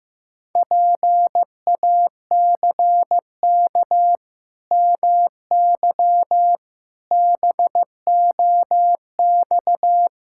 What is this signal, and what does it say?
Morse code, heard as a keyed tone, about 15 wpm, 700 Hz. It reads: PACK MY BOX
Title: Pangram Challenge v2 - 15wpm